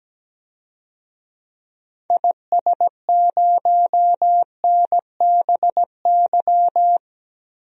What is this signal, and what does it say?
Morse code, heard as a keyed tone, about 17 wpm, 700 Hz. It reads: IS0NBY
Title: DXCC-100 Call Signs - 6 Characters - Repeated 3x - 17wpm